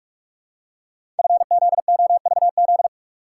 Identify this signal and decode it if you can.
Morse code, heard as a keyed tone, about 44 wpm, 700 Hz. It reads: FZOVZ
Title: Code Groups - 5 Letters - 45wpm